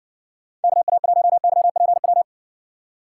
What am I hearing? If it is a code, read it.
Morse code, heard as a keyed tone, about 60 wpm, 700 Hz. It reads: KA1YPW